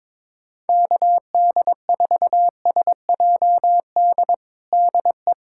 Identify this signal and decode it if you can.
Morse code, heard as a keyed tone, about 22 wpm, 700 Hz. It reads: KD4SJD DE
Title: USA Callsign DE Callsign 22wpm